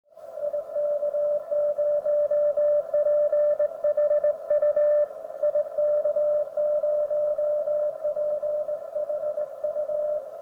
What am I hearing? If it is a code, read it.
Morse code, heard as a keyed tone, about 19 wpm, 590 Hz. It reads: IK0PHU IK0PHU